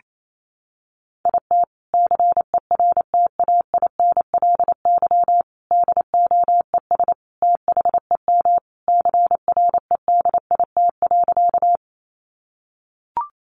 Morse code, heard as a keyed tone, about 28 wpm, 700 Hz. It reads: IT CERTAINLY DOES THEM CREDIT.
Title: Sentences from Top 1000 Words 28wpm